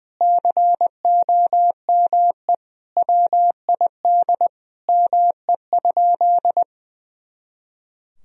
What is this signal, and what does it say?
Morse code, heard as a keyed tone, about 21 wpm, 700 Hz. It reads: COME WID ME?
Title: Sentences from Top 200 Words and Abbreviations 20wpm